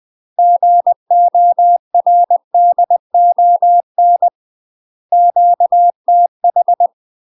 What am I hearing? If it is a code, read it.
Morse code, heard as a keyed tone, about 20 wpm, 700 Hz. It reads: GORDON QTH
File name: UK MASTER QSOs TRAINING / 20WPM / UK MASTER QSOs 0023